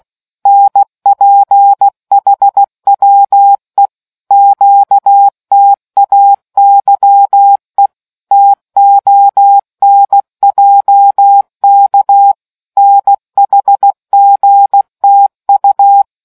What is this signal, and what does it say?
Morse code, heard as a keyed tone, about 16 wpm, 800 Hz. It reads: NPHWE QTAYE TONJK NHGTU